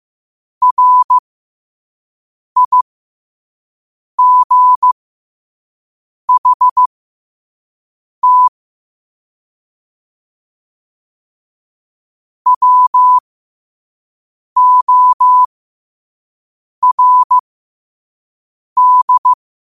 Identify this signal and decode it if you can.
Morse code, heard as a keyed tone, about 15 wpm, spaced as for 5 wpm, 1000 Hz. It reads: RIGHT WORD
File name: qotdfiles / 5wpm